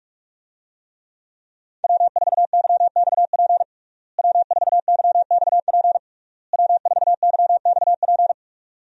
Morse code, heard as a keyed tone, about 46 wpm, 700 Hz. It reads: W4YXP W4YXP W4YXP
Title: USA Call Signs - 5 Characters - Repeated 3x - 45wpm